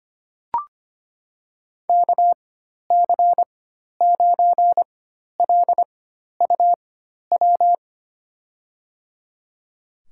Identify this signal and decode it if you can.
Morse code, heard as a keyed tone, about 25 wpm, spaced as for 12 wpm, 700 Hz. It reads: KC9LUW
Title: USA Call Signs - 6 Characters - 4x Character Spacing - 25wpm